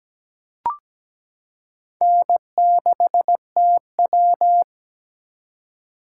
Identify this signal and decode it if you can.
Morse code, heard as a keyed tone, about 17 wpm, 700 Hz. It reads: N6TW
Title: K1USN SST QSO Practice - 17wpm